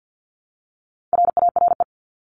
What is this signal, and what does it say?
Morse code, heard as a keyed tone, about 50 wpm, 700 Hz. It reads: RARE
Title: LY Suffix - Top 100 Words with Base Words 50wpm